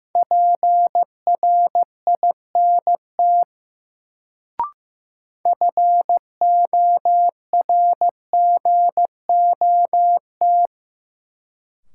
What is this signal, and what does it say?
Morse code, heard as a keyed tone, about 15 wpm, 700 Hz. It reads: PRINT FORGOT
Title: Sentences from Top 1400 Words - New Word Drill - 15wpm